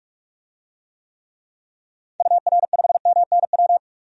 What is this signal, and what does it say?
Morse code, heard as a keyed tone, about 46 wpm, 700 Hz. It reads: UR5MNW
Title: DXCC-100 Call Signs - 6 Characters - 45wpm